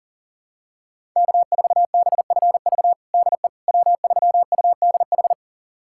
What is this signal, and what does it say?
Morse code, heard as a keyed tone, about 40 wpm, 700 Hz. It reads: K4BFV DE W3UDH
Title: USA Callsign DE Callsign - 5 Characters - 40wpm